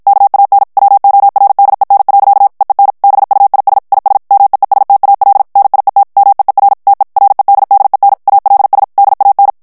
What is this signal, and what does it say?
Morse code, heard as a keyed tone, about 53 wpm, 800 Hz. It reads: KAN WORDEN. EEN BUIS IS MEESTAL NIET MEER TE REDDEN ALS DAN
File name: HOLLANDAIS texte clair 2 vitesse 50 mots minute